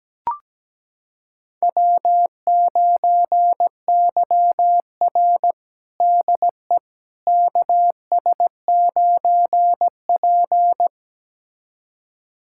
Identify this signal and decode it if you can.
Morse code, heard as a keyed tone, about 17 wpm, 700 Hz. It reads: W9YR DE KS9P
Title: USA Callsign DE Callsign - 4 Characters - 17wpm